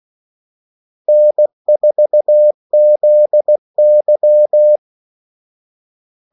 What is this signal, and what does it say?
Morse code, heard as a keyed tone, about 16 wpm, 600 Hz. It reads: N4ZY
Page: Kaldesignaler 16 wpm